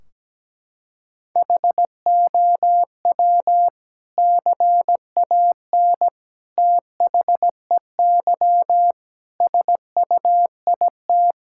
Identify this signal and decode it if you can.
Morse code, heard as a keyed tone, about 17 wpm, 700 Hz. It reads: HOW CAN THEY SUIT
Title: Sentences from Top 1600 Words 17wpm